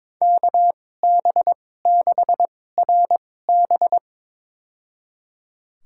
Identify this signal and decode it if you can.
Morse code, heard as a keyed tone, about 22 wpm, spaced as for 16 wpm, 700 Hz. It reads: KB6RB
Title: USA Call Signs - 5 Characters - 2x Character Spacing - 22wpm